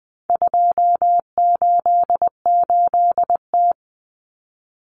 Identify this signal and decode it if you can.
Morse code, heard as a keyed tone, about 20 wpm, 700 Hz. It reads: 288T
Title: Common Cut Numbers - 4-Digit 20wpm